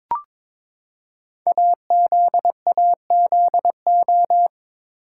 Morse code, heard as a keyed tone, about 22 wpm, 700 Hz. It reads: AZAZO
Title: Code Groups - 5 Letters - 22wpm